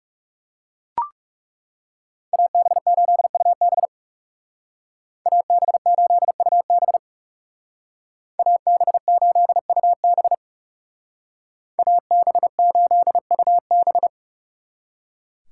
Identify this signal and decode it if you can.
Morse code, heard as a keyed tone, about 36 wpm, 700 Hz. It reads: AB8UB AB8UB AB8UB AB8UB
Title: Speed Racing - USA Call Signs 30wpm